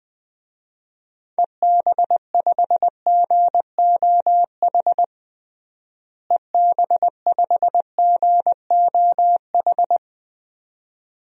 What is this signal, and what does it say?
Morse code, heard as a keyed tone, about 20 wpm, 700 Hz. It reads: EB5GOH EB5GOH
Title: DXCC-100 Call Signs - 6 Characters - Repeated 2x - 20wpm